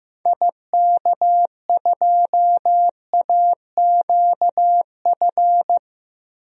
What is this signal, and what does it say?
Morse code, heard as a keyed tone, about 15 wpm, 700 Hz. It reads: IK2AQF
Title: DXCC-100 Call Signs - 6 Characters - 15wpm